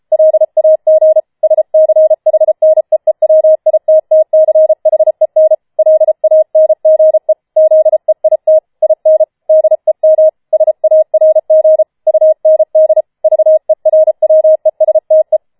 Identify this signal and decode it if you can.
Morse code, heard as a keyed tone, about 33 wpm, 630 Hz. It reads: LAG SCHNEEWITTCHEN LANGE ZEIT IN DEM SARG UND VERWESTE